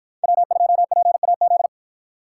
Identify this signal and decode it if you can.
Morse code, heard as a keyed tone, about 48 wpm, 700 Hz. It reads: W2PAZ